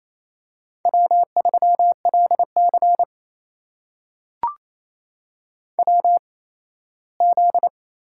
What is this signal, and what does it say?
Morse code, heard as a keyed tone, about 28 wpm, spaced as for 17 wpm, 700 Hz. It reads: W3LC WZ
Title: USA Call Signs - 4 Characters - 8x Character Spacing - 28wpm